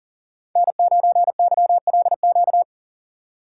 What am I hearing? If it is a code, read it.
Morse code, heard as a keyed tone, about 40 wpm, 700 Hz. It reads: N9YLQ
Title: USA Call Signs - 6x Character Spacing - 40wpm